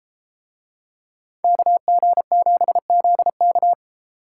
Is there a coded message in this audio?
Yes: Morse code from 1.4 to 3.7 s, heard as a keyed tone, about 33 wpm, 700 Hz. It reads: KG7ZK